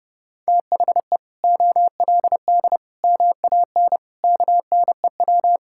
Morse code, heard as a keyed tone, about 30 wpm, 700 Hz. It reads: THE OLD MAN KNEW
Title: Sentences from Top 1000 Words - Review Part 6 - 30wpm